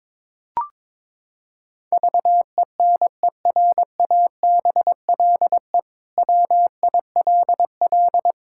Morse code, heard as a keyed tone, about 22 wpm, 700 Hz. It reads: VENERABLE WILL
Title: Pangram Challenge v2 - 22wpm